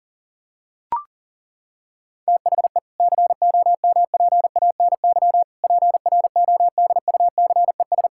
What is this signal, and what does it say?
Morse code, heard as a keyed tone, about 40 wpm, 700 Hz. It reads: THE COMPANY PRODUCES
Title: Sentences from Top 1000 Words - Review Part 6 - 40wpm